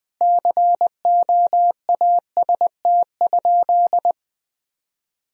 Morse code, heard as a keyed tone, about 21 wpm, 700 Hz. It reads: COAST?
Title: Sentences from Top 1000 Words - Review Part 4 - 20wpm